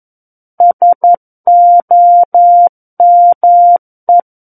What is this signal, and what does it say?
Morse code, heard as a keyed tone, about 11 wpm, 700 Hz. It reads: SOME
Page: SKCC - War of the Worlds - 02-Chapter 2 - 11 WPM